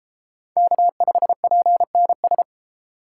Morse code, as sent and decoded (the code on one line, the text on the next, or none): -.- ..... .--. -. ...
K5PNS